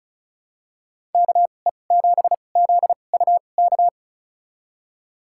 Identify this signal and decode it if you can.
Morse code, heard as a keyed tone, about 36 wpm, spaced as for 25 wpm, 700 Hz. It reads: KE7ZUK
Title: USA Call Signs - 6 Characters - 2x Character Spacing - 35wpm